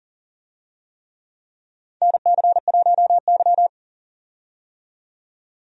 Morse code, heard as a keyed tone, about 40 wpm, 700 Hz. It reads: NC1Y
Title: USA Call Signs - 4 Characters - 40wpm